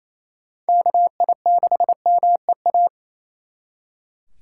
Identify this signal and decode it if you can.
Morse code, heard as a keyed tone, about 36 wpm, 700 Hz. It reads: KI6MEA